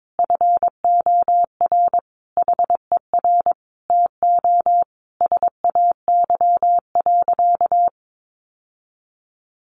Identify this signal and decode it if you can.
Morse code, heard as a keyed tone, about 22 wpm, 700 Hz. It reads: FOR HER TO SAY.
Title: Sentences from Top 1400 Words 22wpm